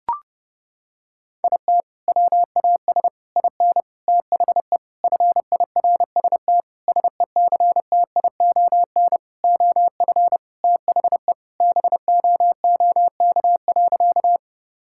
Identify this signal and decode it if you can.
Morse code, heard as a keyed tone, about 30 wpm, 700 Hz. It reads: IT WAS IN THE FIRST SECTION OF THE BOOK.
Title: Sentences from Top 1000 Words - Review Part 5 - 30wpm